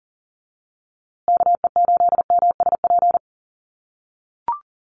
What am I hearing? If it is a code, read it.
Morse code, heard as a keyed tone, about 40 wpm, 700 Hz. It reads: KE8MSP
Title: USA Call Signs Repeated 2x 40wpm